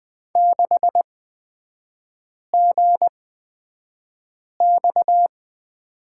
Morse code, heard as a keyed tone, about 20 wpm, spaced as for 5 wpm, 700 Hz. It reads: 6GX
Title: 236 Lesson 38e - Letter q - Three Characters 20wpm